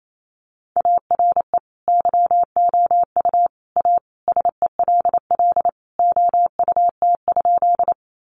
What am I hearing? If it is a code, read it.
Morse code, heard as a keyed tone, about 28 wpm, 700 Hz. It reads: ARE YOU A SELL OUT?
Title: Sentences from Top 1100 Words 28wpm